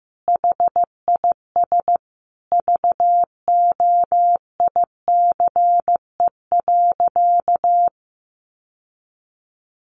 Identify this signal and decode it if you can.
Morse code, heard as a keyed tone, about 15 wpm, 700 Hz. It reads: HIS VOICE.
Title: Sentences from Top 1400 Words 15wpm